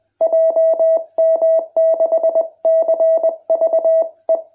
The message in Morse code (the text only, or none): JM6CVE